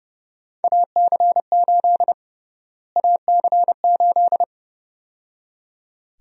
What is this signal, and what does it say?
Morse code, heard as a keyed tone, about 31 wpm, 700 Hz. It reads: AC8 AC8
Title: USA Call Sign Prefixes - Repeated 2x - 30wpm